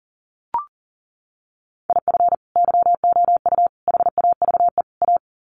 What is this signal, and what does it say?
Morse code, heard as a keyed tone, about 37 wpm, 700 Hz. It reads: IF YOU HAVE A
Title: Sets of 4 Words 40wpm